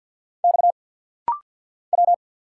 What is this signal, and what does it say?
Morse code, heard as a keyed tone, about 48 wpm, 700 Hz. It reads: X W
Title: Single Letter-Number - Rapid-Fire 50wpm